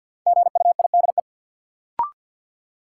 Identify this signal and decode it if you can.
Morse code, heard as a keyed tone, about 51 wpm, 700 Hz. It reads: GUIDE
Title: Sentences from Top 600 Words - New Word Drill - 50wpm